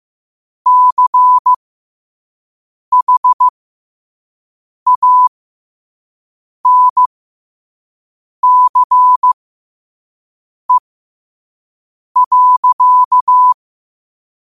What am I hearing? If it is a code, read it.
Morse code, heard as a keyed tone, about 15 wpm, spaced as for 5 wpm, 1000 Hz. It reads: CHANCE.
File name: qotdfiles / 5wpm